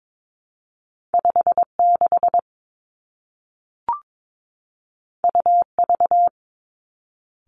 Morse code, heard as a keyed tone, about 22 wpm, 700 Hz. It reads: 5 6 U V
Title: Two Challenging Letter-Number Combinations 22wpm